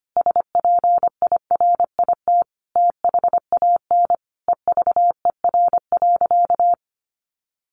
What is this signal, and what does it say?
Morse code, heard as a keyed tone, about 25 wpm, 700 Hz. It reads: SPIRIT THAN EVER.
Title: Sentences from Top 1400 Words 25wpm